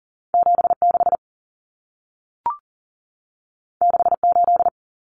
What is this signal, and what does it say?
Morse code, heard as a keyed tone, about 40 wpm, 700 Hz. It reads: 76 68
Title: Numbers - 2-Digit 40wpm